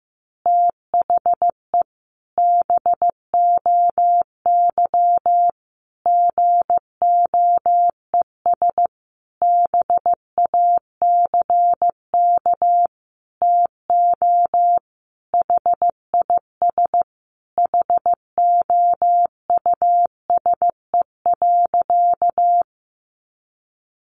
Morse code, heard as a keyed tone, about 15 wpm, 700 Hz. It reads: THE BOY GOES BACK TO HIS HOUSE.
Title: Additional Easy Phrases 15wpm